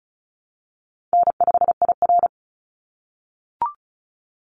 Morse code, heard as a keyed tone, about 34 wpm, 700 Hz. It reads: N5IR
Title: Call Signs - 35wpm